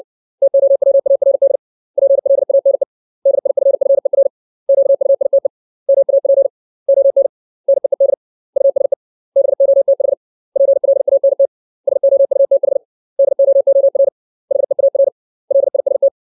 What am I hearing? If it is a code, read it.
Morse code, heard as a keyed tone, about 60 wpm, 550 Hz. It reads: TOWARD PLANE DIFFER QUITE MAP ON NEED USE BOTH PLANT SOUTH DOOR HEAR LIST
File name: 632 words / 632 Words 60WPM